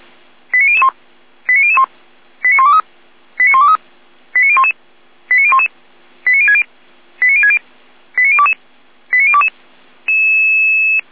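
Five-tone selective call (ZVEI), heard as quick runs of five tones, tones of about 80 ms. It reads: 89001 89001 89123 89123 89011 89011 89060 89060 89022 89022
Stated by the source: ZVEI five-tone call received on SDR.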